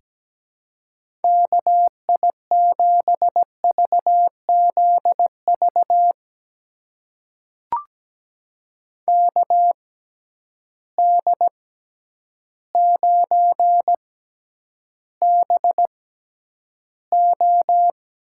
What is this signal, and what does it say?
Morse code, heard as a keyed tone, about 17 wpm, spaced as for 9 wpm, 700 Hz. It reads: KI7VZV KD9BO
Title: USA Call Signs - 6 Characters - 6x Character Spacing - 17wpm